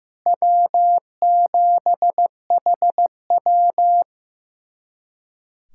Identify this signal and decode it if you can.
Morse code, heard as a keyed tone, about 15 wpm, 700 Hz. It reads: W7HW
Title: USA Callsign DE Callsign - 4 Characters - 15wpm